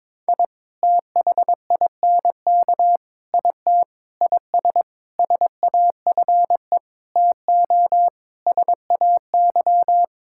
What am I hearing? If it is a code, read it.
Morse code, heard as a keyed tone, about 22 wpm, 700 Hz. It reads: I THINK IT IS SAFE TO SAY
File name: practice / 22wpm / Sets of 8 Words - 22wpm